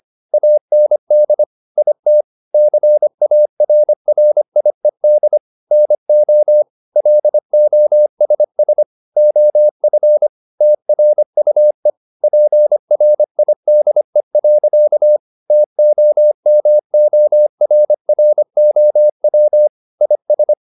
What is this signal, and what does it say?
Morse code, heard as a keyed tone, about 25 wpm, 600 Hz. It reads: AND IT CARRIED NO LOSS OF TRUE PRIDE. TOMORROW IS
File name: the old man and the sea part 1 25wpm